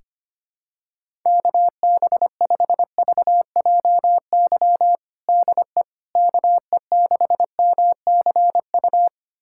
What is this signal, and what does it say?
Morse code, heard as a keyed tone, about 25 wpm, 700 Hz. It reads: KB5VJY DE KE6MCU
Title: USA Callsign DE Callsign 25wpm